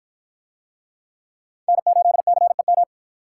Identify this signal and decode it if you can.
Morse code, heard as a keyed tone, about 53 wpm, 700 Hz. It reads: N8CEM